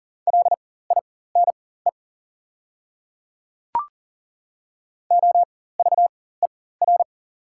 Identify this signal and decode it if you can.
Morse code, heard as a keyed tone, about 40 wpm, spaced as for 19 wpm, 690 Hz. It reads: LINE OVER
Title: 4x Character Spacing - Top 200 Words - 40wpm